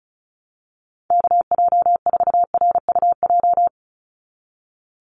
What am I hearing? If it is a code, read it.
Morse code, heard as a keyed tone, about 34 wpm, 700 Hz. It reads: KJ4RUJ
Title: USA Call Signs 35wpm